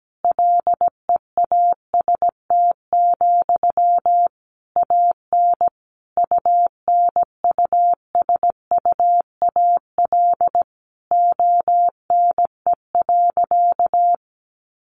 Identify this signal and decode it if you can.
Morse code, heard as a keyed tone, about 17 wpm, 700 Hz. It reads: LEAST, AN UNUSUAL ONE.
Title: Sentences from Top 1400 Words 17wpm